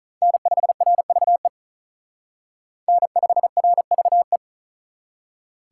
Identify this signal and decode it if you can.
Morse code, heard as a keyed tone, about 44 wpm, 700 Hz. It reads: N5RVE N5RVE